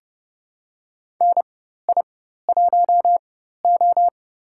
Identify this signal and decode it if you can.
Morse code, heard as a keyed tone, about 30 wpm, spaced as for 14 wpm, 700 Hz. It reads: NI1O
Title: USA Call Signs - 4 Characters - 4x Character Spacing - 30wpm